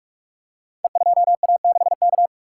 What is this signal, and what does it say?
Morse code, heard as a keyed tone, about 44 wpm, 700 Hz. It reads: E2A6K
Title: Code Groups - 5 Letter:Numbers - 45wpm